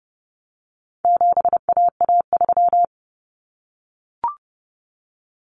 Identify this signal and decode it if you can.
Morse code, heard as a keyed tone, about 30 wpm, 700 Hz. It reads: 7AA3
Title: Common Cut Numbers - 4-Digit 30wpm